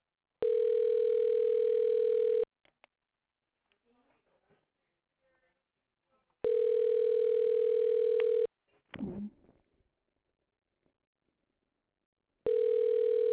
Ringback tone (North America), 440 + 480 Hz.